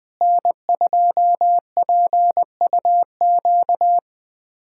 Morse code, heard as a keyed tone, about 21 wpm, 690 Hz. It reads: N2PUQ
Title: USA Callsign DE Callsign 20wpm